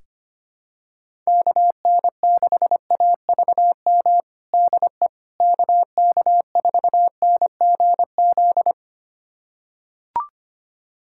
Morse code, heard as a keyed tone, about 25 wpm, 700 Hz. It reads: KN6AVM DE KK4NGZ
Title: USA Callsign DE Callsign 25wpm